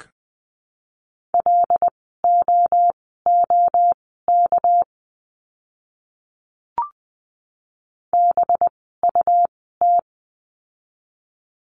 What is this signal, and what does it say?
Morse code, heard as a keyed tone, about 20 wpm, spaced as for 14 wpm, 700 Hz. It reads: LOOK BUT
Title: 237 Lesson 38f - Letter q - Words 20wpm